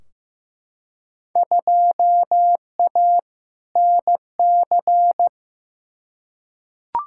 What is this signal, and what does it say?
Morse code, heard as a keyed tone, about 15 wpm, 700 Hz. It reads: 2A NC
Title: Field Day Exchanges Encore - 15wpm